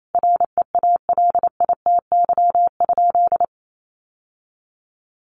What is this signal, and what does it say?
Morse code, heard as a keyed tone, about 28 wpm, 700 Hz. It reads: REALITY?
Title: Sentences from Top 1200 Words 28wpm